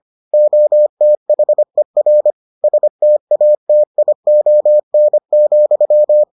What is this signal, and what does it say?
Morse code, heard as a keyed tone, about 25 wpm, 600 Hz. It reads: OTHER STATION,